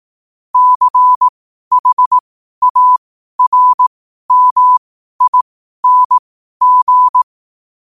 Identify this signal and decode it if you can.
Morse code, heard as a keyed tone, about 18 wpm, spaced as for 12 wpm, 1000 Hz. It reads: CHARMING
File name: qotdfiles / 12wpm